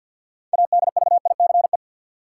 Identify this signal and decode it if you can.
Morse code, heard as a keyed tone, about 53 wpm, 700 Hz. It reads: ADVICE
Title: Six Letter Words - 50wpm